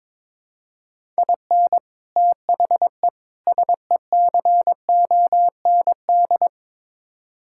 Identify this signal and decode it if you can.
Morse code, heard as a keyed tone, about 22 wpm, 700 Hz. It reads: IN THE SECOND
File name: practice / 22wpm / Build A Phrase - 5-Word Phrases - 22wpm